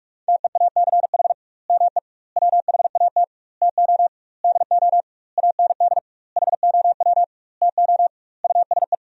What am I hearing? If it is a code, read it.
Morse code, heard as a keyed tone, about 43 wpm, 700 Hz. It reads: TEACH ME WHAT TO DO AND HOW TO USE
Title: 2x Word-Spacing - Sentences from Top 500 Words - 45wpm